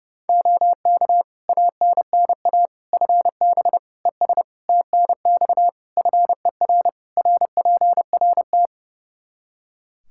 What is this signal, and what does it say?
Morse code, heard as a keyed tone, about 30 wpm, 700 Hz. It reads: OK ANNA FB ES TNX FER RPRT
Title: Practice QSO - Exchange 2 - Info - 30wpm